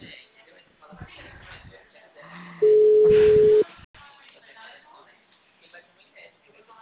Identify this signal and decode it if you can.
Ringback tone (Europe, 425 Hz), with speech.
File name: out- -2010- - - .24399